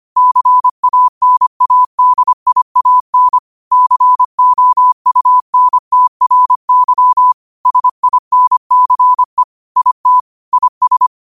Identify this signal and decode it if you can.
Morse code, heard as a keyed tone, about 25 wpm, 1000 Hz. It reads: CANADIAN COUNTRY SINCE IT IS
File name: qotdfiles / 25wpm